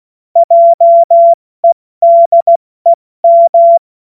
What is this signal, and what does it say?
Morse code, heard as a keyed tone, about 16 wpm, 680 Hz. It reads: JEDEM
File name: Morsekurs / Witze / Witz 06. September 2022 70 F80